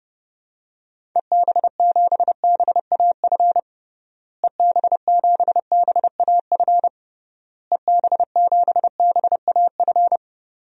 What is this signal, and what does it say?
Morse code, heard as a keyed tone, about 30 wpm, 700 Hz. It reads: EB7BAF EB7BAF EB7BAF
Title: DXCC-100 Call Signs - 6 Characters - Repeated 3x - 30wpm